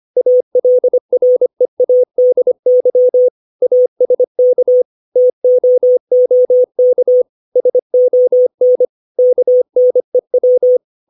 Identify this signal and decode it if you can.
Morse code, heard as a keyed tone, about 25 wpm, 500 Hz. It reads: ALREADY ASK TOOK SON KNEW
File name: 632 words / 632 Words 25WPM